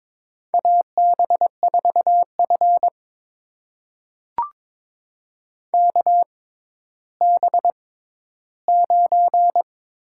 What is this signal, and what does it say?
Morse code, heard as a keyed tone, about 22 wpm, spaced as for 12 wpm, 700 Hz. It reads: AB4F KB9
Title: USA Call Signs - 6x Character Spacing - 22wpm